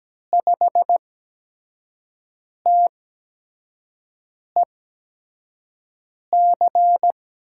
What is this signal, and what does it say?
Morse code, heard as a keyed tone, about 17 wpm, spaced as for 5 wpm, 700 Hz. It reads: 5TEC
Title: USA Call Signs - 6 Characters - 8x Character Spacing - 17wpm